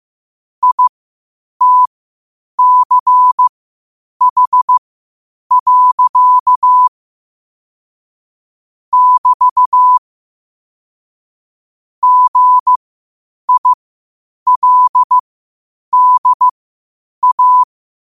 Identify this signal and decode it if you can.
Morse code, heard as a keyed tone, about 15 wpm, spaced as for 8 wpm, 1000 Hz. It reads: ITCH. <BT> GILDA
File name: 8wpm